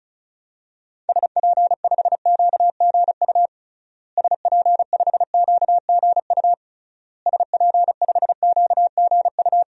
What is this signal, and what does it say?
Morse code, heard as a keyed tone, about 36 wpm, 700 Hz. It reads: SP5QGU SP5QGU SP5QGU
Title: DXCC-100 Call Signs - 6 Characters - Repeated 3x - 35wpm